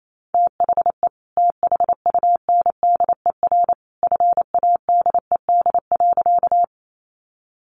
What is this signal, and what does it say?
Morse code, heard as a keyed tone, about 28 wpm, 700 Hz. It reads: THE THUNDER FADED.
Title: A Sound Of Thunder 28wpm